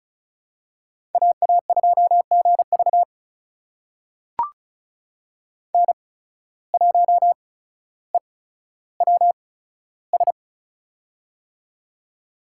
Morse code, heard as a keyed tone, about 34 wpm, spaced as for 15 wpm, 700 Hz. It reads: AA2GV N1EWS
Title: USA Call Signs - 5 Characters - 8x Character Spacing - 35wpm